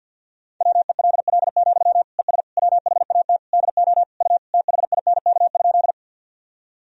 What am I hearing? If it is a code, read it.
Morse code, heard as a keyed tone, about 52 wpm, 700 Hz. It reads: WELL, ES WHAT DO U THINK?
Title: Sentences from Top 200 Words and Abbreviations 50wpm